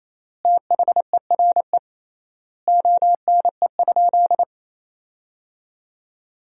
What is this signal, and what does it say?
Morse code, heard as a keyed tone, about 28 wpm, 700 Hz. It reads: THERE ONE?
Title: 3x Word-Spacing - Sentences from Top 100 Words - 28wpm